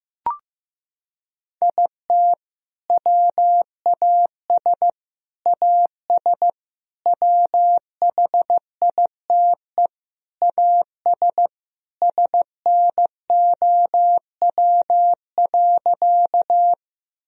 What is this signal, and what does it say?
Morse code, heard as a keyed tone, about 15 wpm, 700 Hz. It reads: IT WAS AS WHITE AS SNOW.
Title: Sentences from Top 1000 Words - Review Part 3 - 15wpm